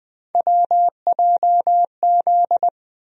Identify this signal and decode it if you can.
Morse code, heard as a keyed tone, about 21 wpm, 700 Hz. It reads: WJZ
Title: USA Call Sign Postfixes - 3 Letters 20wpm